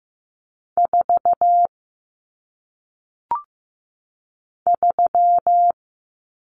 Morse code, heard as a keyed tone, about 15 wpm, 700 Hz. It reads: 4 3
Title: Numbers - 1-Digit 15wpm